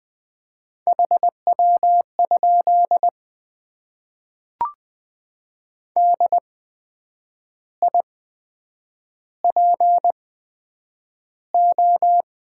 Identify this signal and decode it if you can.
Morse code, heard as a keyed tone, about 20 wpm, spaced as for 8 wpm, 700 Hz. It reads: HW? DIPO
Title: Top 90 QSO Elements - 8x Character Spacing - 20wpm